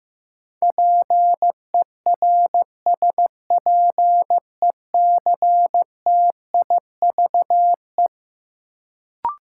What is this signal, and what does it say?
Morse code, heard as a keyed tone, about 15 wpm, 700 Hz. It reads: PERSPECTIVE
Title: Sentences from Top 1200 Words - New Word Drill - 15wpm